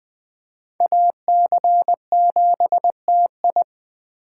Morse code, heard as a keyed tone, about 20 wpm, 700 Hz. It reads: AC7TI
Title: Speed Racing - USA Call Signs 17wpm